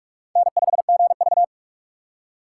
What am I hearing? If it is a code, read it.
Morse code, heard as a keyed tone, about 46 wpm, 700 Hz. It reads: N5GV
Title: USA Call Signs - 4 Characters - 2x Character Spacing - 45wpm